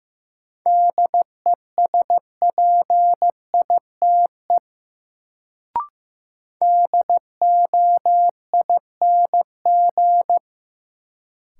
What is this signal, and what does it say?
Morse code, heard as a keyed tone, about 15 wpm, 700 Hz. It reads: DESPITE DOING
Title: Sets of 4 Words - New Word Drill - 15wpm